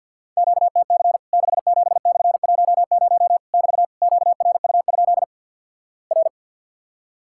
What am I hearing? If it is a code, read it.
Morse code, heard as a keyed tone, about 50 wpm, 700 Hz. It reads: QTC 67/10 <BT> QRU? R
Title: Worked All Europe DX Contest - QTC Exchange - 50wpm